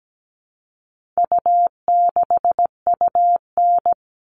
Morse code, heard as a keyed tone, about 17 wpm, 700 Hz. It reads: U6UN
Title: All Cut Numbers - 4-Digit 17wpm